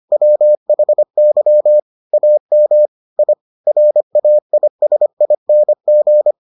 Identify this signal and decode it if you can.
Morse code, heard as a keyed tone, about 25 wpm, 600 Hz. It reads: WHY AM I RAISING